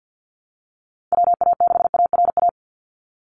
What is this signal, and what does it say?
Morse code, heard as a keyed tone, about 50 wpm, 700 Hz. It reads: WA6ARA
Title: NA SOTA Chaser Callsigns 50wpm